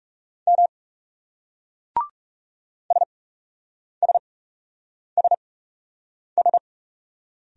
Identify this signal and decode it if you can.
Morse code, heard as a keyed tone, about 40 wpm, spaced as for 8 wpm, 700 Hz. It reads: M SSSS